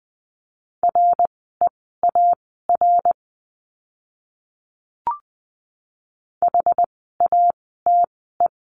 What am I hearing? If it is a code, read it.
Morse code, heard as a keyed tone, about 20 wpm, spaced as for 15 wpm, 700 Hz. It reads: REAR HATE
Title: 061 Lesson 11f - Letter h - Words 20wpm